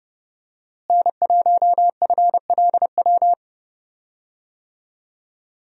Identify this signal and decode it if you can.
Morse code, heard as a keyed tone, about 30 wpm, 700 Hz. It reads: N1FLW
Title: USA Call Signs - 5 Characters - Repeated 3x - 30wpm